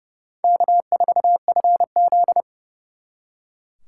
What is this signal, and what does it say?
Morse code, heard as a keyed tone, about 38 wpm, 700 Hz. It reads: K4FZ